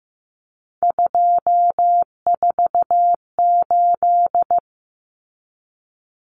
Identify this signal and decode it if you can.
Morse code, heard as a keyed tone, about 15 wpm, 700 Hz. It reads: 248
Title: Numbers - 3-Digit 15wpm